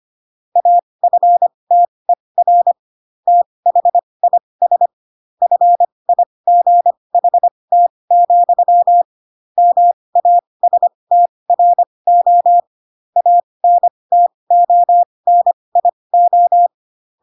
Morse code, heard as a keyed tone, about 25 wpm, spaced as for 20 wpm, 700 Hz. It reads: AFTER THIS FIGHT, MASTRO ANTONIO
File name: Pinocchio / Pinocchio ch2 15 25 ik1hge